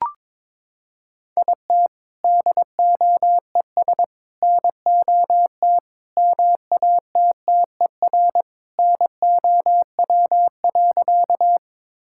Morse code, heard as a keyed tone, about 22 wpm, 700 Hz. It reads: IT DOES NOT MATTER NOW.
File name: practice / 22wpm / Sentences with 4-Word Phrases - 22wpm